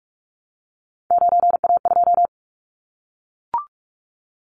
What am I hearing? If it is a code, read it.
Morse code, heard as a keyed tone, about 44 wpm, 700 Hz. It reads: 9A2
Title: Callsign Prefixes for Top 100 DXCC - 45wpm